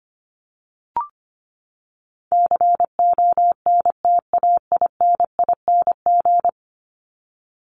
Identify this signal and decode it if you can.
Morse code, heard as a keyed tone, about 25 wpm, 700 Hz. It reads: CONTAINING
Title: ING Suffix - Top 100 Words with Base Words 25wpm